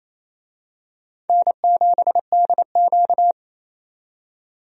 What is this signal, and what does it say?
Morse code, heard as a keyed tone, about 28 wpm, 700 Hz. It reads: N7DQ
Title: USA Call Signs - 4 Characters - 28wpm